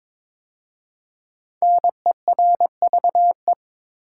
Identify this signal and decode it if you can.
Morse code, heard as a keyed tone, about 22 wpm, 700 Hz. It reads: NERVE
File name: 5 Letter Words - First 3 Letters Given - 22wpm